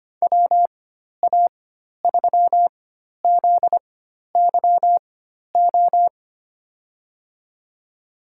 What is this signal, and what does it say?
Morse code, heard as a keyed tone, about 25 wpm, spaced as for 12 wpm, 700 Hz. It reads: WA3ZYO
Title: USA Call Signs - 6 Characters - 4x Character Spacing - 25wpm